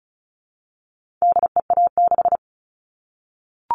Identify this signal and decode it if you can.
Morse code, heard as a keyed tone, about 34 wpm, 700 Hz. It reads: DEA6
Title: All Cut Numbers - 4-Digit 35wpm